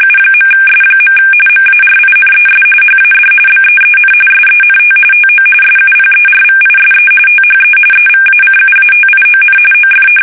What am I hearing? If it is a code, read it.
RTTY, about 45.45 baud. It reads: U.S. MARINES HAD BEEN SENT TO SUPERVISE THE EVACUATION OF
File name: fsk850 / battleshipnj